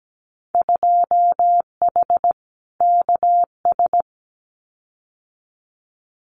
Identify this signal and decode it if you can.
Morse code, heard as a keyed tone, about 17 wpm, 700 Hz. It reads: 2H KS
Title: Winter Field Day Exchanges Encore 17wpm